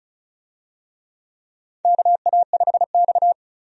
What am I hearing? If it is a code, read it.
Morse code, heard as a keyed tone, about 34 wpm, 700 Hz. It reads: KA5X
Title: USA Call Signs - 4 Characters - 35wpm